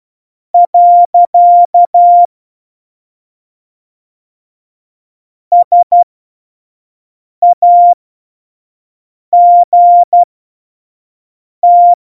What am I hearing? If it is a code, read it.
Morse code, heard as a keyed tone, about 12 wpm, spaced as for 5 wpm, 700 Hz. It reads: . SAGT
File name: Morsekurs / Witze / Witz 03 April 2018, 25 F70